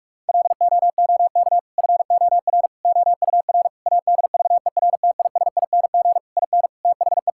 Morse code, heard as a keyed tone, about 46 wpm, 700 Hz. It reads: LOOK FOR OUR ADVERTISING IN THE
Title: Sentences from Top 1000 Words - Review Part 6 - 45wpm